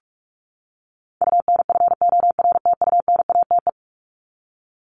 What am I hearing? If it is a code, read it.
Morse code, heard as a keyed tone, about 44 wpm, 700 Hz. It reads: UNFORTUNATE
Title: LY Suffix - Top 100 Words with Base Words 45wpm